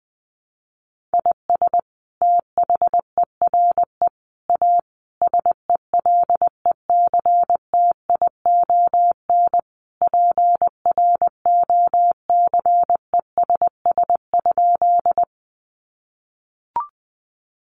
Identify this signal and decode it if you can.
Morse code, heard as a keyed tone, about 20 wpm, 700 Hz. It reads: IS THERE A SELECTION PROCESS?
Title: Sentences from Top 1400 Words 20wpm